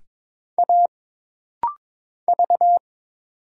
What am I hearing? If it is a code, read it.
Morse code, heard as a keyed tone, about 22 wpm, 700 Hz. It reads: A V
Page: This rapid-fire timing is geared to push your ICR (Instant Character Recognition) skills.